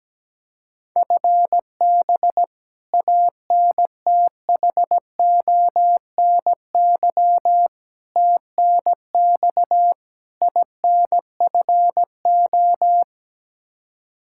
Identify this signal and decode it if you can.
Morse code, heard as a keyed tone, about 17 wpm, 700 Hz. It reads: FB ANTHONY TNX INFO
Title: Practice QSO - Exchange 3 - More Info - 17wpm